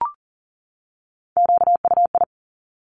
Morse code, heard as a keyed tone, about 38 wpm, 700 Hz. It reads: QUI
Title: Top Three Letter Combinations in Words 40wpm